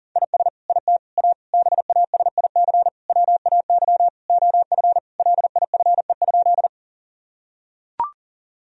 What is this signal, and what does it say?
Morse code, heard as a keyed tone, about 37 wpm, 700 Hz. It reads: IS IT A BASIC WAY OF LIFE?
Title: Sentences from Top 1000 Words - Review Part 3 - 40wpm